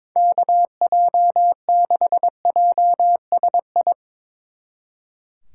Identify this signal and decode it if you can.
Morse code, heard as a keyed tone, about 24 wpm, 700 Hz. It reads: KJ6JSI
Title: Speed Racing - USA 2x3 Call Signs 22wpm